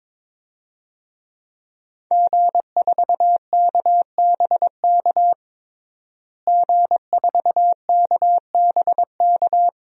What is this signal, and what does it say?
Morse code, heard as a keyed tone, about 22 wpm, 700 Hz. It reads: G4KBK G4KBK
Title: DXCC-100 Call Signs - 5 Characters - Repeated 2x - 22wpm